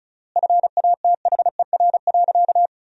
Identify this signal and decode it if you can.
Morse code, heard as a keyed tone, about 36 wpm, 700 Hz. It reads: FATHER.
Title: 3x Word-Spacing - Sentences from Top 300 Words - 35wpm